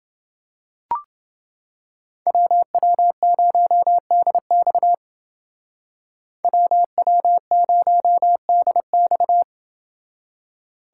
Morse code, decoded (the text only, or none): WW0DX WW0DX